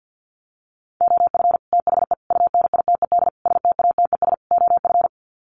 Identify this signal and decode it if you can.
Morse code, heard as a keyed tone, about 50 wpm, 700 Hz. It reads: OF THE UNITED STATES OF
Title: Sets of 5 Words Encore 50wpm